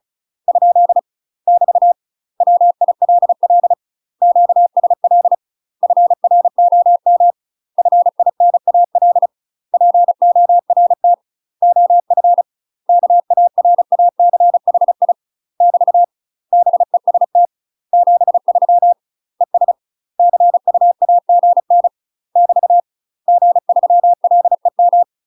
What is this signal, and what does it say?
Morse code, heard as a keyed tone, about 34 wpm, 700 Hz. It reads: ? <BT> WILL QSL FROM FINAL PORT OF KARACHI <BT> BEST 73 ES CUAGN <BT> G3LEM
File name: UK MASTER QSOs TRAINING/35WPM